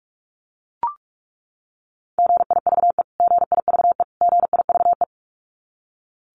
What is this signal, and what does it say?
Morse code, heard as a keyed tone, about 44 wpm, 700 Hz. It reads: GIVE GIVE GIVE
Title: Top 200 Words Repeated 45wpm